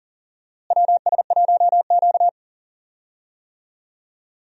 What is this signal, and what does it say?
Morse code, heard as a keyed tone, about 39 wpm, 700 Hz. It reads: WS1Q
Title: USA Call Signs - 4 Characters - Repeated 3x - 40wpm